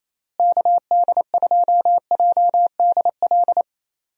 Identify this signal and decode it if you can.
Morse code, heard as a keyed tone, about 28 wpm, 700 Hz. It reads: KD2JDL
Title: USA Call Signs - 6 Characters - Repeated 3x - 28wpm